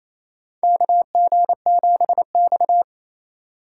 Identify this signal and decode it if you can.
Morse code, heard as a keyed tone, about 27 wpm, 700 Hz. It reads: KG7X
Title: USA Call Signs - 4 Characters - 3x Character Spacing - 28wpm